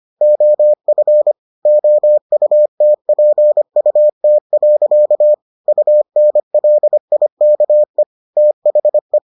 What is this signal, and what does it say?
Morse code, heard as a keyed tone, about 25 wpm, 600 Hz. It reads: OF OUTPUT. UNLIKE THE